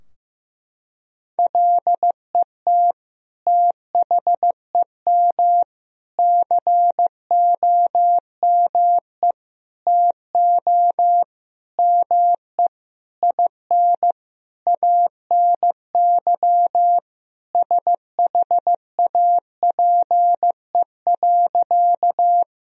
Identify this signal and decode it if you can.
Morse code, heard as a keyed tone, about 15 wpm, 700 Hz. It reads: LET THEM COME TO ME IN ANY SHAPE.
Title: Sentences from Top 1000 Words - Review Part 3 - 15wpm